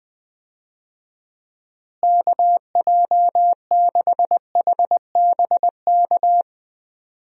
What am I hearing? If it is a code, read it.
Morse code, heard as a keyed tone, about 20 wpm, 700 Hz. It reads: KJ6HBK
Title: USA Call Signs - 6 Characters - Repeated 3x - 20wpm